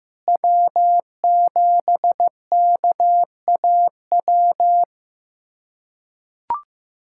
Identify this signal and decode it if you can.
Morse code, heard as a keyed tone, about 15 wpm, 700 Hz. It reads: W7KAW
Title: USA Callsign DE Callsign - 5 Characters - 15wpm